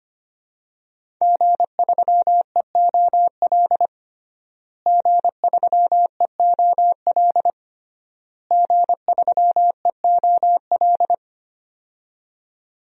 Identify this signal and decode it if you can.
Morse code, heard as a keyed tone, about 25 wpm, 700 Hz. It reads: G3EOL G3EOL G3EOL
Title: DXCC-100 Call Signs - 5 Characters - Repeated 3x - 25wpm